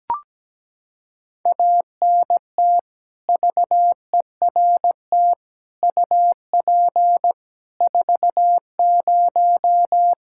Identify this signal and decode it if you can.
Morse code, heard as a keyed tone, about 17 wpm, 700 Hz. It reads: ANT VERT UP 40
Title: Practice QSO - Exchange 2 - Info - 17wpm